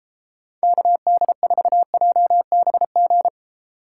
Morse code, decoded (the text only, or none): KD4JBG